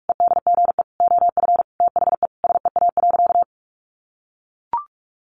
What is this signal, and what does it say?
Morse code, heard as a keyed tone, about 44 wpm, 700 Hz. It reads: EDGE OF THE SEA.
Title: Sentences from Top 1300 Words 45wpm